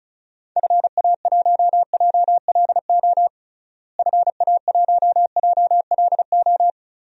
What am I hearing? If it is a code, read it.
Morse code, heard as a keyed tone, about 36 wpm, 700 Hz. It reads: FA1JLO FA1JLO
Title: DXCC-100 Call Signs - 6 Characters - Repeated 2x - 35wpm